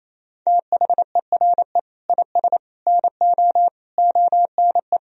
Morse code, heard as a keyed tone, about 28 wpm, 700 Hz. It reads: THERE IS NO ONE
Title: Sentences from Top 1000 Words - Review Part 3 - 28wpm